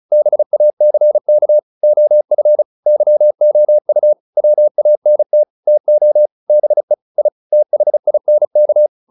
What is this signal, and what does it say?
Morse code, heard as a keyed tone, about 35 wpm, 600 Hz. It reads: BACK OF YOU WANT TO BE I THINK